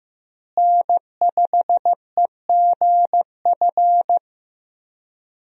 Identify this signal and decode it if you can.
Morse code, heard as a keyed tone, about 15 wpm, 700 Hz. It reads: N5EGF
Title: USA Call Signs - 5 Characters - Repeated 2x - 15wpm